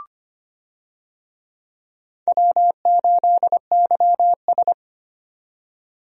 Morse code, heard as a keyed tone, about 25 wpm, 700 Hz. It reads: W8YS
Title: USA Call Signs - 4 Characters - 25wpm